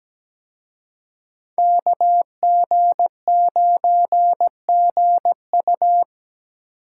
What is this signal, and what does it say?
Morse code, heard as a keyed tone, about 20 wpm, 700 Hz. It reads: KG9GU
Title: Speed Racing - USA 2x2 Call Signs 17wpm